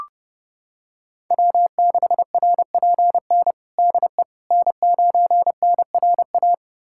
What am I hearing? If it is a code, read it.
Morse code, heard as a keyed tone, about 30 wpm, 700 Hz. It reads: W6RPN DE N9NRA
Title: USA Callsign DE Callsign - 5 Characters - 30wpm